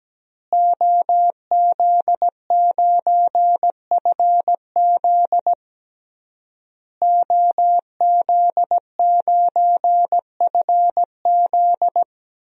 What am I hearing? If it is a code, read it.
Morse code, heard as a keyed tone, about 17 wpm, 700 Hz. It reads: OZ9FZ OZ9FZ
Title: DXCC-100 Call Signs - 5 Characters - Repeated 3x - 17wpm